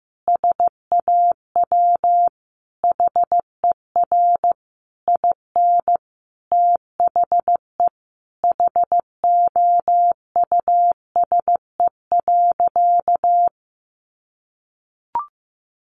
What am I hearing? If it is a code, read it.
Morse code, heard as a keyed tone, about 15 wpm, 700 Hz. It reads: SAW HER IN THE HOUSE.
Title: Sentences from Top 800 Words 15wpm